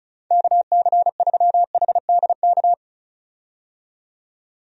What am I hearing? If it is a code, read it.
Morse code, heard as a keyed tone, about 36 wpm, 700 Hz. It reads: KC3HDK
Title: USA Call Signs - 6 Characters - 35wpm